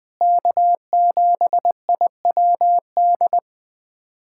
Morse code, heard as a keyed tone, about 21 wpm, 700 Hz. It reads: K7IWD
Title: USA Callsign DE Callsign - 5 Characters - 20wpm